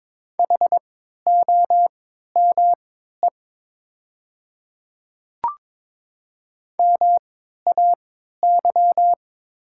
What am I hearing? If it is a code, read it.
Morse code, heard as a keyed tone, about 21 wpm, spaced as for 12 wpm, 700 Hz. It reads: HOME MAY
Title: 3x Character Spacing - Top 500 Words - 22wpm